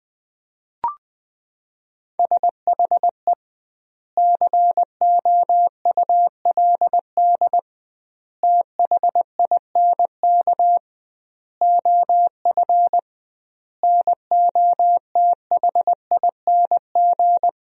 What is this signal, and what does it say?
Morse code, heard as a keyed tone, about 20 wpm, 700 Hz. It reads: SHE COULD THINK OF NOTHING
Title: 2x Word-Spacing - Sentences from Top 500 Words - 20wpm